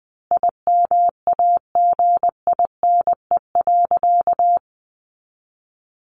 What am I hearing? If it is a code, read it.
Morse code, heard as a keyed tone, about 20 wpm, 700 Hz. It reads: IMAGINE.
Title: Sentences from Top 1400 Words 20wpm